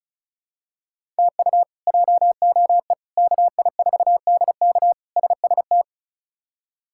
Morse code, heard as a keyed tone, about 34 wpm, 700 Hz. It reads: TU JOE KI4DK SST
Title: K1USN SST QSO Practice - 35wpm